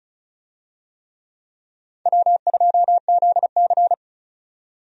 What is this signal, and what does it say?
Morse code, heard as a keyed tone, about 36 wpm, 700 Hz. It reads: W2ZC
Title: USA Call Signs - 4 Characters - 35wpm